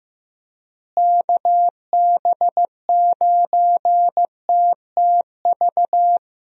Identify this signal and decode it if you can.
Morse code, heard as a keyed tone, about 16 wpm, 700 Hz. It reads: KB9TTV
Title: USA Call Signs - 6 Characters - Repeated 3x - 15wpm